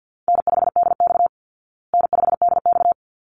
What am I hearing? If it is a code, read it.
Morse code, heard as a keyed tone, about 50 wpm, 700 Hz. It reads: N5DX N5DX
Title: Call Signs Repeated 3x - 50wpm